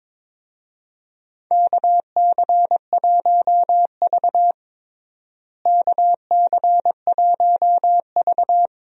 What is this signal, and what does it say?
Morse code, heard as a keyed tone, about 21 wpm, 700 Hz. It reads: KC1V KC1V
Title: USA Call Signs - 4 Characters - Repeated 2x - 22wpm